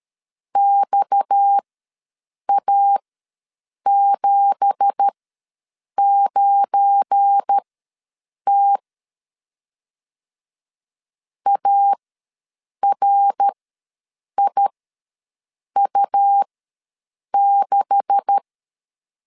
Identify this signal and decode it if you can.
Morse code, heard as a keyed tone, about 13 wpm, spaced as for 7 wpm, 800 Hz. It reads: XA79T ARIU6
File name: LEZIONI / 9 2867 ws 12wpm 3431